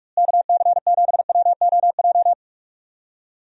Morse code, heard as a keyed tone, about 46 wpm, 700 Hz. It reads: KC7WOJ
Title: USA Call Signs - 6 Characters - Repeated 3x - 45wpm